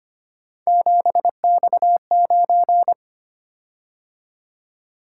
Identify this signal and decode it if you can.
Morse code, heard as a keyed tone, about 25 wpm, 700 Hz. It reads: 7X9
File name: Three Letter-Number Combinations - 25wpm